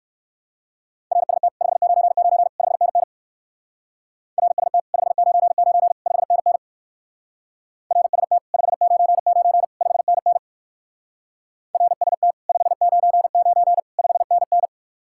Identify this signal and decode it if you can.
Morse code, heard as a keyed tone, about 57 wpm, 700 Hz. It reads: RST 599 5NN RST 599 5NN RST 599 5NN RST 599 5NN